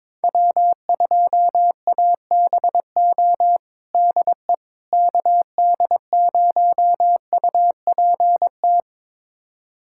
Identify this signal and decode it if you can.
Morse code, heard as a keyed tone, about 23 wpm, 700 Hz. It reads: W2ABO DE KD0UPT
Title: USA Callsign DE Callsign 22wpm